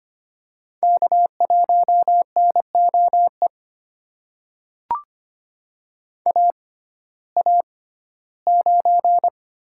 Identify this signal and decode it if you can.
Morse code, heard as a keyed tone, about 25 wpm, spaced as for 15 wpm, 700 Hz. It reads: K1NOE AA9
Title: USA Call Signs - 6x Character Spacing - 25wpm